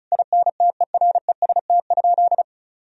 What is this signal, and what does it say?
Morse code, heard as a keyed tone, about 36 wpm, 700 Hz. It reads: INTEREST?
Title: Sentences from Top 1000 Words - Review Part 2 - 35wpm